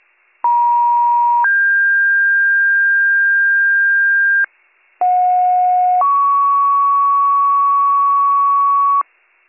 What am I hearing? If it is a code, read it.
Two-tone page, heard as a one-second tone, then a three-second tone, about 950 then 1600 Hz, about 730 then 1120 Hz, with noise. It is heard from 0.4 to 9.0 s.